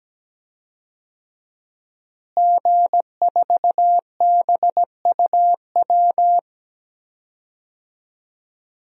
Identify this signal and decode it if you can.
Morse code, heard as a keyed tone, about 17 wpm, 700 Hz. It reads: G4BUW
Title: DXCC-100 Call Signs - 5 Characters - Repeated 3x - 17wpm